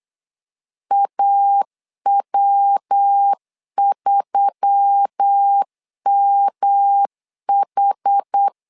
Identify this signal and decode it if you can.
Morse code, heard as a keyed tone, about 8 wpm, 800 Hz. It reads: AW3MH
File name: LEZIONI / 4 Q1S3 8wpm 3037